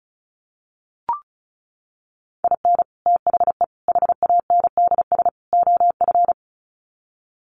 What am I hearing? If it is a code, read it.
Morse code, heard as a keyed tone, about 34 wpm, 700 Hz. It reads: IN THE HANDS OF
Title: Sets of 4 Words 35wpm